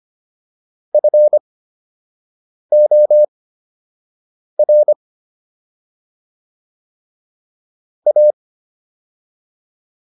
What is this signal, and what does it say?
Morse code, heard as a keyed tone, about 25 wpm, spaced as for 6 wpm, 600 Hz. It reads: FOR A